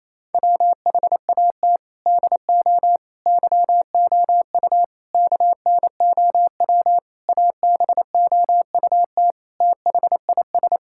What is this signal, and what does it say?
Morse code, heard as a keyed tone, about 28 wpm, 700 Hz. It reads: WHAT DO YOU KNOW ABOUT THIS
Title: Sentences from Top 1000 Words - Review Part 6 - 28wpm